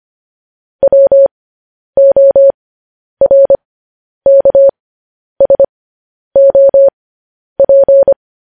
Morse code, heard as a keyed tone, about 25 wpm, spaced as for 10 wpm, 550 Hz. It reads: WORKSOP
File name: podcast / News headlines-1x-25x10